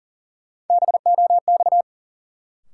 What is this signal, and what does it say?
Morse code, heard as a keyed tone, about 40 wpm, 700 Hz. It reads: BOX
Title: Sentences from Top 500 Words - New Word Drill - 40wpm